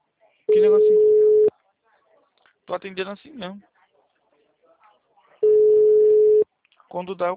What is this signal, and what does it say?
Ringback tone (Europe, 425 Hz), with speech.